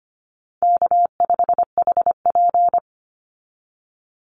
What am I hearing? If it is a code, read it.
Morse code, heard as a keyed tone, about 25 wpm, 700 Hz. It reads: K5HP
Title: Call Signs Repeated 3x - 25wpm